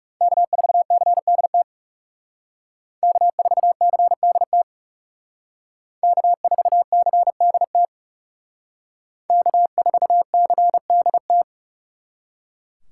Morse code, heard as a keyed tone, about 36 wpm, 700 Hz. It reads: K4CDT K4CDT K4CDT K4CDT
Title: Speed Racing - USA 1x3 Call Signs 30wpm